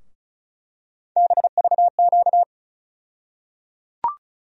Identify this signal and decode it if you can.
Morse code, heard as a keyed tone, about 36 wpm, 700 Hz. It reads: BVQ
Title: USA Call Sign Postfixes - 3 Letters 35wpm